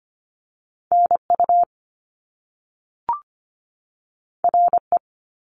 Morse code, heard as a keyed tone, about 25 wpm, 700 Hz. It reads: NU RE
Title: Top Two Letter Combinations in Words 25wpm